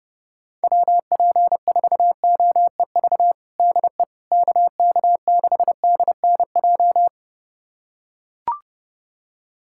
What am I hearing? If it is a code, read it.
Morse code, heard as a keyed tone, about 30 wpm, 700 Hz. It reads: WP4OEV DE KK6DNJ
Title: USA Callsign DE Callsign - 6 Characters - 30wpm